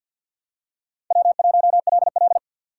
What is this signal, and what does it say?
Morse code, heard as a keyed tone, about 47 wpm, 700 Hz. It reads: W1LL
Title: USA Call Signs - 4 Characters - 3x Character Spacing - 50wpm